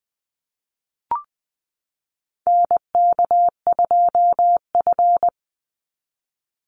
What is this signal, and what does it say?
Morse code, heard as a keyed tone, about 20 wpm, 700 Hz. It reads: NK2F
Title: Call Signs Repeated 3x - 20wpm